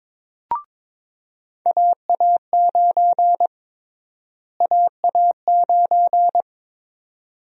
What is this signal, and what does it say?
Morse code, heard as a keyed tone, about 22 wpm, 700 Hz. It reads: AA9 AA9
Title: USA Call Sign Prefixes - Repeated 2x - 22wpm